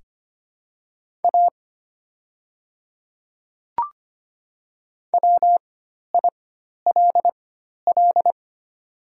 Morse code, heard as a keyed tone, about 25 wpm, spaced as for 12 wpm, 700 Hz. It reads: A WILL
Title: 4x Character Spacing - Top 100 Words - 25wpm